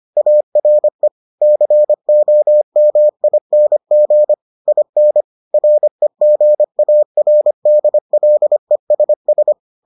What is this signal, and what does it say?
Morse code, heard as a keyed tone, about 25 wpm, 600 Hz. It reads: ARE COMING IN REGARDLESS